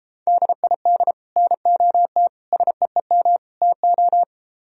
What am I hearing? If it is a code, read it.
Morse code, heard as a keyed tone, about 33 wpm, 700 Hz. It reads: DID NOT SEEM TO